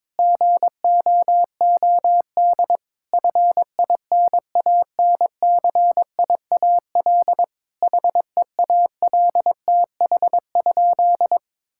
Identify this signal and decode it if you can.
Morse code, heard as a keyed tone, about 22 wpm, 700 Hz. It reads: GOOD FINANCIAL HEALTH?
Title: Sentences from Top 1000 Words - Review Part 5 - 22wpm